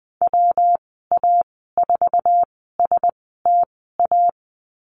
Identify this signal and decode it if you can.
Morse code, heard as a keyed tone, about 20 wpm, spaced as for 15 wpm, 700 Hz. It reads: WA4STA
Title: 186 Lesson 30g - Letter b - Call Signs 20wpm